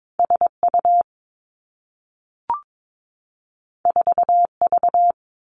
Morse code, heard as a keyed tone, about 22 wpm, 700 Hz. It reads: S U 4 V
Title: Two Challenging Letter-Number Combinations 22wpm